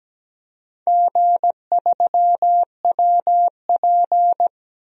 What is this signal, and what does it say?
Morse code, heard as a keyed tone, about 17 wpm, 700 Hz. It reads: G3WP
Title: DXCC-100 Call Signs - 4 Characters - 17wpm